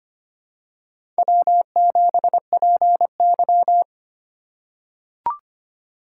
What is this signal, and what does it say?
Morse code, heard as a keyed tone, about 25 wpm, 700 Hz. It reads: W7PY
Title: USA Call Signs - 4 Characters - 3x Character Spacing - 25wpm